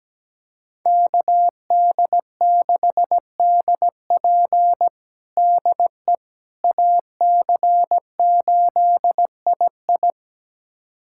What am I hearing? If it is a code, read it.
Morse code, heard as a keyed tone, about 17 wpm, 700 Hz. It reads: KD6DP DE AC8II
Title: USA Callsign DE Callsign - 5 Characters - 17wpm